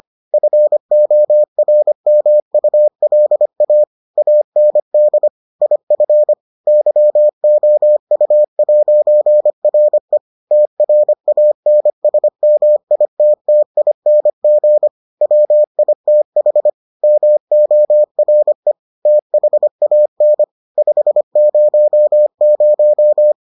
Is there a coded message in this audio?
Yes: Morse code from 0.3 to 23.3 s, heard as a keyed tone, about 25 wpm, 600 Hz. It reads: FORMULA AND IF YOU'RE TRANSMITTING WITH MORE THAN 500